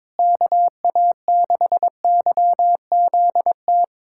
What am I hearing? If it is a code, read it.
Morse code, heard as a keyed tone, about 22 wpm, 700 Hz. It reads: KA6YZT
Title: USA Callsign DE Callsign - 6 Characters - 22wpm